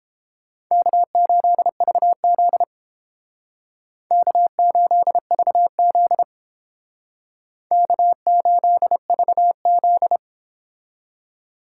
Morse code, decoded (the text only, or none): K8VZ K8VZ K8VZ